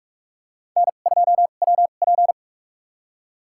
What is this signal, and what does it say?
Morse code, heard as a keyed tone, about 41 wpm, spaced as for 31 wpm, 700 Hz. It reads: N2WP
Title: USA Call Signs - 2x Character Spacing - 45wpm